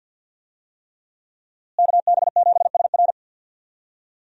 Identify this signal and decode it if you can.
Morse code, heard as a keyed tone, about 48 wpm, 700 Hz. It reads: KB7SR
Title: USA Call Signs - 5 Characters - Repeated 3x - 50wpm